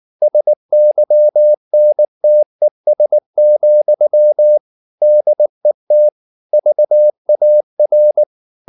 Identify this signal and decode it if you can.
Morse code, heard as a keyed tone, about 19 wpm, 600 Hz. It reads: SYNTES, DET VAR
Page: Ferie 19 wpm